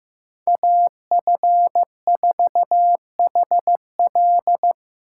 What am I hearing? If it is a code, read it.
Morse code, heard as a keyed tone, about 18 wpm, 700 Hz. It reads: AF4HL
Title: Speed Racing - USA 2x2 Call Signs 15wpm